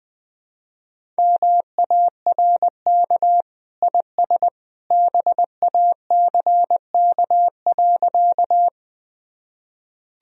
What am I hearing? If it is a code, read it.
Morse code, heard as a keyed tone, about 20 wpm, 700 Hz. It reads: MARK IS BACK.
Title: Sentences from Top 1000 Words - Review Part 1 - 20wpm